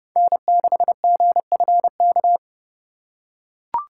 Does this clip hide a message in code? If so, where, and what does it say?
Yes: Morse code from 0.2 to 2.4 s, heard as a keyed tone, about 28 wpm, 700 Hz. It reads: N6GFK